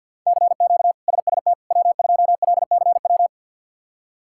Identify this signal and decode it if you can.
Morse code, heard as a keyed tone, about 48 wpm, 690 Hz. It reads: CQ SST W2LCW
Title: K1USN SST QSO Practice - 50wpm